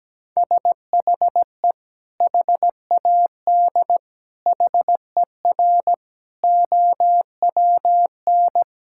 Morse code, heard as a keyed tone, about 17 wpm, 700 Hz. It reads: SHE HAD HER OWN
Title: Sentences from Top 1000 Words - Review Part 5 - 17wpm